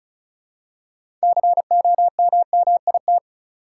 Morse code, heard as a keyed tone, about 36 wpm, 700 Hz. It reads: COMMIT
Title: Six Letter Words - 35wpm